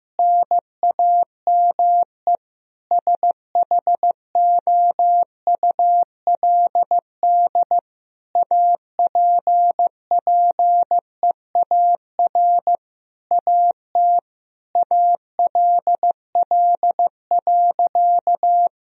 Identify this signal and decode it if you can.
Morse code, heard as a keyed tone, about 16 wpm, 700 Hz. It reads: NAME SHOULD APPEAR AT ALL.
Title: Sentences from Top 1000 Words - Review Part 2 - 15wpm